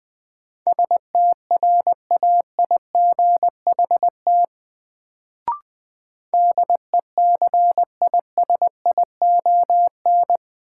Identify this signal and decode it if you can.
Morse code, heard as a keyed tone, about 21 wpm, 700 Hz. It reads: STRAIGHT DECISION
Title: Sentences from Top 1000 Words - New Word Drill - 20wpm